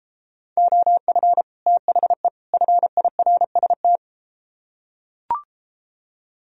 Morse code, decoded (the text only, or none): OF THE FIRST